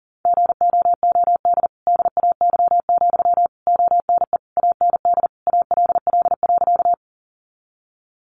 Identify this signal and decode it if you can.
Morse code, heard as a keyed tone, about 40 wpm, 700 Hz. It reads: GOOD DAY, ONE AND ALL.
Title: Sentences from Top 200 Words 40wpm